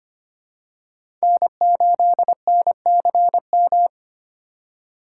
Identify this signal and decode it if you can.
Morse code, heard as a keyed tone, about 25 wpm, 700 Hz. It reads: N8NCM
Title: USA Call Signs - 5 Characters - 4x Character Spacing - 25wpm